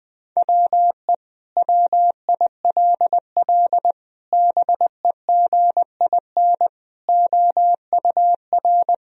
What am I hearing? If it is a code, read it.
Morse code, heard as a keyed tone, about 20 wpm, 700 Hz. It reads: WE WILL BEGIN OUR
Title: Sentences from Top 1000 Words - Review Part 4 - 20wpm